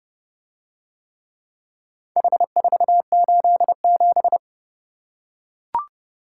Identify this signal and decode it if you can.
Morse code, heard as a keyed tone, about 31 wpm, 700 Hz. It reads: H487